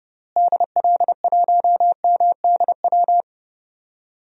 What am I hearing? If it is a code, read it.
Morse code, heard as a keyed tone, about 30 wpm, 700 Hz. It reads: DL1MDW
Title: DXCC-100 Call Signs - 6 Characters - Repeated 2x - 30wpm